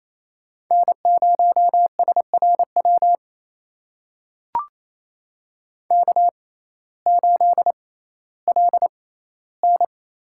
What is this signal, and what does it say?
Morse code, heard as a keyed tone, about 28 wpm, spaced as for 15 wpm, 700 Hz. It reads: N0SRW K8LN
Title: USA Call Signs - 5 Characters - 6x Character Spacing - 28wpm